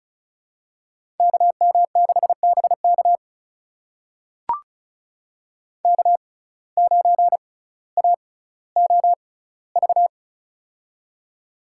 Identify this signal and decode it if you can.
Morse code, heard as a keyed tone, about 36 wpm, spaced as for 18 wpm, 700 Hz. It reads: KM6BK K9AOV
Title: USA Call Signs - 5 Characters - 6x Character Spacing - 35wpm